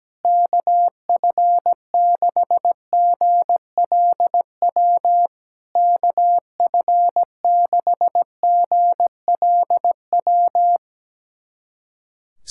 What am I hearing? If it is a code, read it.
Morse code, heard as a keyed tone, about 17 wpm, 700 Hz. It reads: KF6GLW KF6GLW
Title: SOTA QSO Practice 17wpm